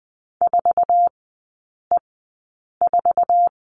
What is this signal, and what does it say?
Morse code, heard as a keyed tone, about 20 wpm, spaced as for 8 wpm, 700 Hz. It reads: 4E4
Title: 047 Lesson 9e - Number 4 - Three Characters 20wpm